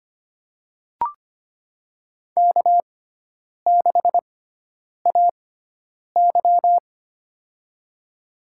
Morse code, heard as a keyed tone, about 25 wpm, spaced as for 9 wpm, 700 Hz. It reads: K6AY
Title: USA Call Signs - 6x Character Spacing - 25wpm